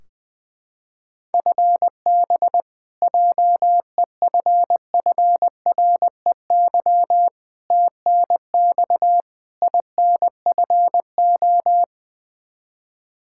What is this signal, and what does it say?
Morse code, heard as a keyed tone, about 20 wpm, 700 Hz. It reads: FB JEFFREY TNX INFO
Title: K1USN SST QSO Practice 20wpm